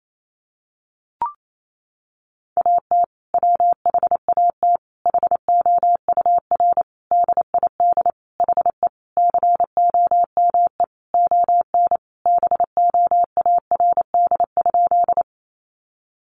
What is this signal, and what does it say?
Morse code, heard as a keyed tone, about 28 wpm, 700 Hz. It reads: AT WHAT HOUR DID HE COME ON BOARD?
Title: Sentences from Top 700 Words 28wpm